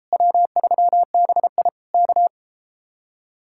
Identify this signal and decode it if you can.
Morse code, heard as a keyed tone, about 33 wpm, 700 Hz. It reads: W3BI K